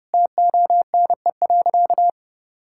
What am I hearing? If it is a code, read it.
Morse code, heard as a keyed tone, about 30 wpm, 700 Hz. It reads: TONE.
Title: Sentences from Top 1600 Words 30wpm